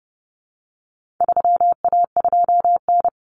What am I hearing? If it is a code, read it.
Morse code, heard as a keyed tone, about 30 wpm, 700 Hz. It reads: 3A2N
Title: Common Cut Numbers - 4-Digit 30wpm